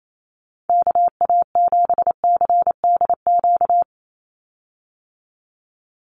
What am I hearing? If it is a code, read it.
Morse code, heard as a keyed tone, about 28 wpm, 700 Hz. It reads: KA7CDQ
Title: USA Call Signs 28wpm